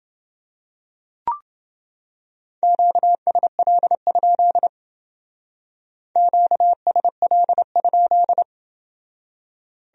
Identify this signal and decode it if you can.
Morse code, heard as a keyed tone, about 28 wpm, 700 Hz. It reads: QSL? QSL?